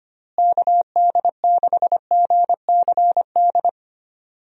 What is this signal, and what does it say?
Morse code, heard as a keyed tone, about 25 wpm, 700 Hz. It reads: KD6GCD
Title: USA Call Signs - 6 Characters - Repeated 2x - 25wpm